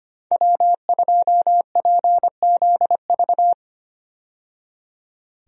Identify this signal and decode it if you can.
Morse code, heard as a keyed tone, about 25 wpm, 700 Hz. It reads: W2PZV
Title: USA Call Signs - 5 Characters - 25wpm